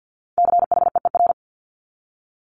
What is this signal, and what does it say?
Morse code, heard as a keyed tone, about 50 wpm, 700 Hz. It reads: CHEER
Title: Multi-letter Phonemes Repeated 3x 50wpm